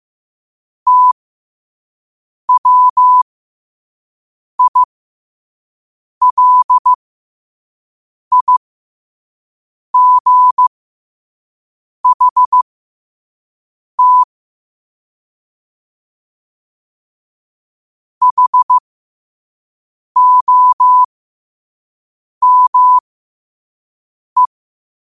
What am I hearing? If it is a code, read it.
Morse code, heard as a keyed tone, about 15 wpm, spaced as for 5 wpm, 1000 Hz. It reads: TWILIGHT HOME